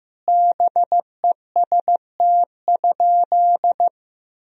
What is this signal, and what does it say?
Morse code, heard as a keyed tone, about 15 wpm, 700 Hz. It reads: BEST?
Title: Sentences from Top 1000 Words - Review Part 2 - 15wpm